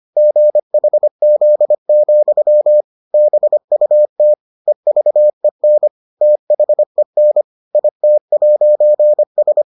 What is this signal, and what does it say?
Morse code, heard as a keyed tone, about 25 wpm, 600 Hz. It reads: GHZ, BUT EVEN THEN IT'S